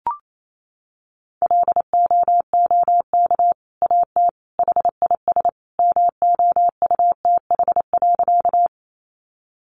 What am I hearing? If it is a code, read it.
Morse code, heard as a keyed tone, about 28 wpm, 700 Hz. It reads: LOOK AT HIS MOUTH.
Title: Sentences from Top 1500 Words 28wpm